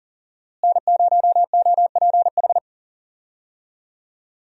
Morse code, heard as a keyed tone, about 40 wpm, 700 Hz. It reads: N0OPH
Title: USA Call Signs - 5 Characters - Repeated 3x - 40wpm